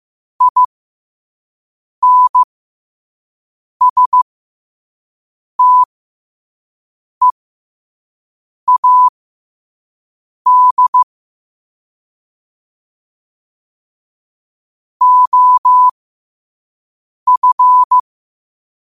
Morse code, heard as a keyed tone, about 15 wpm, spaced as for 5 wpm, 1000 Hz. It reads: INSTEAD OF
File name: qotdfiles / 5wpm